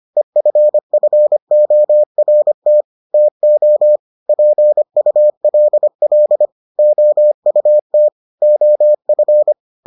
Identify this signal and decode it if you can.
Morse code, heard as a keyed tone, about 25 wpm, 600 Hz. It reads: EFFORT TO PULL OUT OF